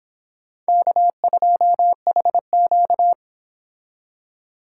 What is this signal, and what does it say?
Morse code, heard as a keyed tone, about 26 wpm, 700 Hz. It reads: K2HQ
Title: Speed Racing - USA Call Signs 22wpm